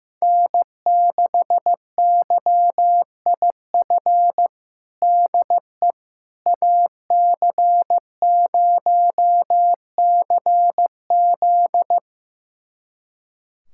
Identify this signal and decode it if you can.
Morse code, heard as a keyed tone, about 15 wpm, 700 Hz. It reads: N6YIF DE AC0CZ
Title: USA Callsign DE Callsign - 5 Characters - 15wpm